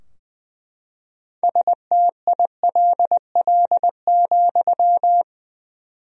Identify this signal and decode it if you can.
Morse code, heard as a keyed tone, about 20 wpm, 700 Hz. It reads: STILL,
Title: 4x Word-Spacing - Sentences from Top 300 Words - 20wpm